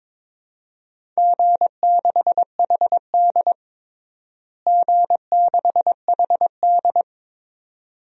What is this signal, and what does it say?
Morse code, heard as a keyed tone, about 21 wpm, 700 Hz. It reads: G6HD G6HD
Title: DXCC-100 Call Signs - 4 Characters - Repeated 2x - 22wpm